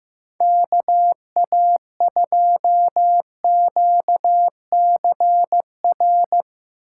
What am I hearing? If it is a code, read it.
Morse code, heard as a keyed tone, about 16 wpm, 700 Hz. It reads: KA2QCR
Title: Speed Racing - USA 2x3 Call Signs 15wpm